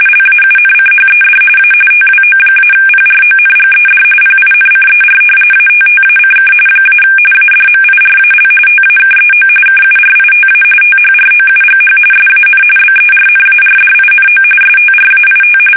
RTTY, about 45.45 baud. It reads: LAUNCHED A TWO-PRONGED PINCER ATTACK - ADMIRAL TAKEO KURITA THROUGH SAN BERNARDINO STRAIT
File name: fsk850 / battleshipnj